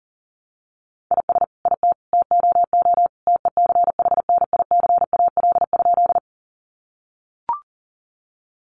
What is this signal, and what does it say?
Morse code, heard as a keyed tone, about 40 wpm, 700 Hz. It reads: IS IT TOO TECHNICAL?
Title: Sentences from Top 1100 Words 40wpm